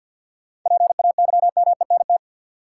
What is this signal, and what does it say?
Morse code, heard as a keyed tone, about 53 wpm, 700 Hz. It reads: PAYMENT
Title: Sentences from Top 1200 Words - New Word Drill - 50wpm